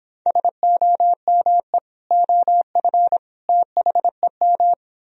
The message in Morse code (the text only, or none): SOME OF THEM